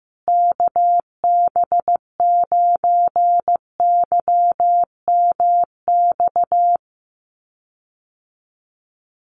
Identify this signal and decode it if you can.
Morse code, heard as a keyed tone, about 14 wpm, 700 Hz. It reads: KB9YMX
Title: USA Call Signs Repeated 3x 15wpm